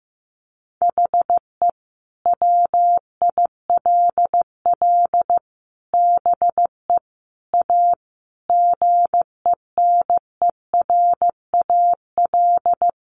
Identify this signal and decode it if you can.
Morse code, heard as a keyed tone, about 15 wpm, 700 Hz. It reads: HE WILL BE A GENERAL
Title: Sentences from Top 1200 Words 15wpm